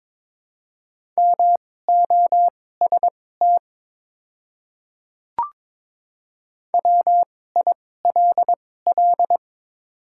Morse code, heard as a keyed tone, about 22 wpm, spaced as for 16 wpm, 690 Hz. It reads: MOST WILL
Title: 2x Character Spacing - Top 100 Words - 22wpm